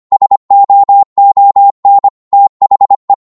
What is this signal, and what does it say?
Morse code, heard as a keyed tone, about 25 wpm, 800 Hz. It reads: SOON THE
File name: 25wpm